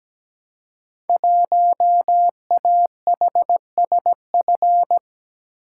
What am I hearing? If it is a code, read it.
Morse code, heard as a keyed tone, about 17 wpm, 700 Hz. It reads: 1AHSF
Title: Code Groups - 5 Letter:Numbers - 17wpm